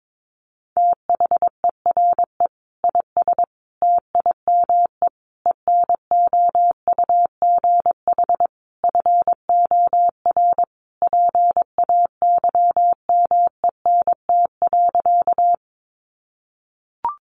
Morse code, heard as a keyed tone, about 22 wpm, 700 Hz. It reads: THERE IS TIME ENOUGH FOR PAYMENT.
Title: Sentences from Top 1200 Words 22wpm